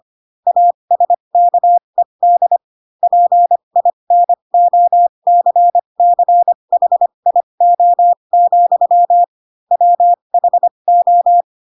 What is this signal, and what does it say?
Morse code, heard as a keyed tone, about 25 wpm, spaced as for 22 wpm, 700 Hz. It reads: ASKED PINOCCHIO, WHO
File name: Pinocchio / Pinocchio ch4 18 25 ik1hge